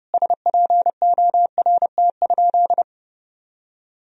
Morse code, heard as a keyed tone, about 30 wpm, 700 Hz. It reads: SPORT?
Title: Sentences from Top 1000 Words - Review Part 4 - 30wpm